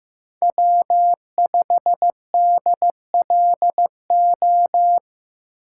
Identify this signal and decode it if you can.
Morse code, heard as a keyed tone, about 15 wpm, 700 Hz. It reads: W5DLO
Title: USA Call Signs - 3x Character Spacing - 15wpm